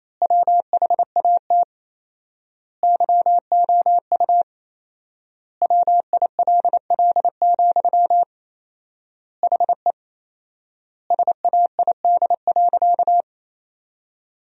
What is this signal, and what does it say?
Morse code, heard as a keyed tone, about 28 wpm, 700 Hz. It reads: WHAT YOU WILL, HE SAID.
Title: 4x Word-Spacing - Sentences from Top 200 Words - 28wpm